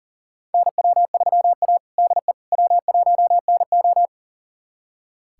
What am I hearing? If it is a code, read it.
Morse code, heard as a keyed tone, about 38 wpm, 700 Hz. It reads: NW3A DE W1NO
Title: USA Callsign DE Callsign - 4 Characters - 40wpm